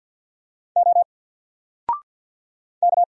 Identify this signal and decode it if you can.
Morse code, heard as a keyed tone, about 57 wpm, 700 Hz. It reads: O K
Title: Single Letter-Number Alt 50wpm